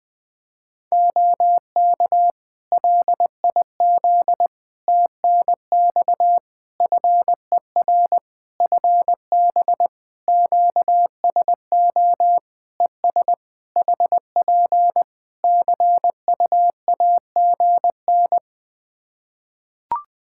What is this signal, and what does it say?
Morse code, heard as a keyed tone, about 20 wpm, 700 Hz. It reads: OK LIZ TNX FER FB QSO ES HP CUAGN
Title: K1USN SST QSO Practice 20wpm